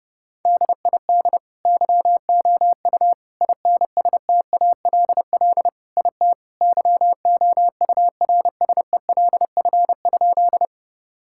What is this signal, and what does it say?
Morse code, heard as a keyed tone, about 32 wpm, 700 Hz. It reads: DID YOU INSTALL IT YOURSELF?
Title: Sentences from Top 1600 Words 30wpm